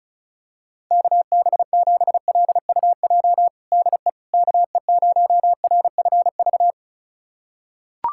Morse code, heard as a keyed tone, about 34 wpm, 700 Hz. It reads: KB7LUJ DE KE0RFV
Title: USA Callsign DE Callsign - 6 Characters - 35wpm